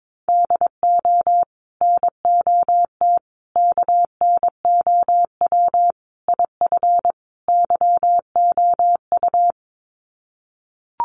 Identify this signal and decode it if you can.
Morse code, heard as a keyed tone, about 22 wpm, 700 Hz. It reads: DO NOT KNOW IF YOU
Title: Sets of 5 Words 22wpm